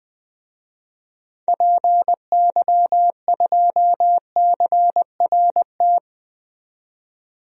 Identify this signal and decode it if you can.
Morse code, heard as a keyed tone, about 20 wpm, 700 Hz. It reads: PY2CRT
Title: DXCC-100 Call Signs - 6 Characters - 20wpm